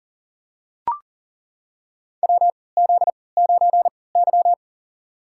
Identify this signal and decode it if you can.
Morse code, heard as a keyed tone, about 40 wpm, spaced as for 23 wpm, 700 Hz. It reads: WZ9Y
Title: USA Call Signs - 4 Characters - 3x Character Spacing - 40wpm